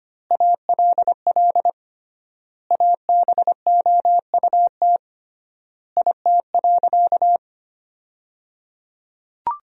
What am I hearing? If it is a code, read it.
Morse code, heard as a keyed tone, about 25 wpm, 700 Hz. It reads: ALL ABOUT IT.
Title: 3x Word-Spacing - Sentences from Top 400 Words - 25wpm